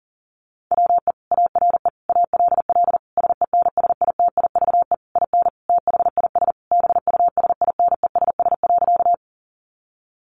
Morse code, heard as a keyed tone, about 40 wpm, 700 Hz. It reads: WE ARE ALL SENSITIVE IN THIS BUSINESS.
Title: Sentences from Top 1400 Words 40wpm